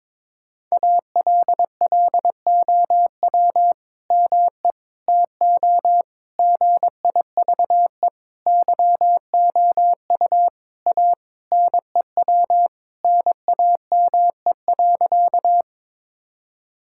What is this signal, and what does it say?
Morse code, heard as a keyed tone, about 22 wpm, 700 Hz. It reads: ALLOW ME TO GIVE YOU A NEW NAME.
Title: Sentences from Top 1000 Words - Review Part 5 - 22wpm